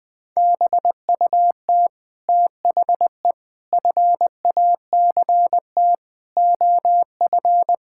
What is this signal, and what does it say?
Morse code, heard as a keyed tone, about 20 wpm, 700 Hz. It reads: BUT THE FACT OF
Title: Sentences with 9-Word Phrases - 20wpm